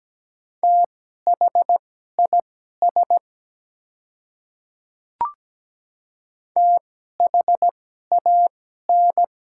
Morse code, heard as a keyed tone, about 17 wpm, spaced as for 12 wpm, 700 Hz. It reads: THIS THAN
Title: 2x Character Spacing - Top 100 Words - 17wpm